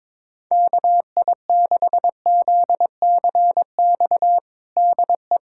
Morse code, heard as a keyed tone, about 22 wpm, 700 Hz. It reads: KI6ZCX DE
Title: USA Callsign DE Callsign - 6 Characters - 22wpm